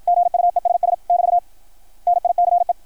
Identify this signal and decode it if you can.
Morse code, heard as a keyed tone, about 53 wpm, 690 Hz. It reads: GUESS <BT> NICE
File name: archived / CRSnet